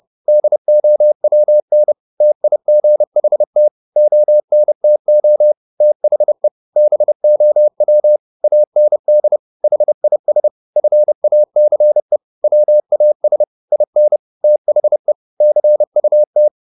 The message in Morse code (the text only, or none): DOWN TIGHT ONTO THE BOW AND HIS FACE WAS IN THE CUT